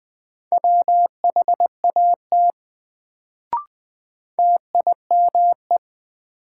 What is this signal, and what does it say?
Morse code, heard as a keyed tone, about 20 wpm, 700 Hz. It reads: WHAT TIME
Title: Sets of 2 Words - New Word Drill - 20wpm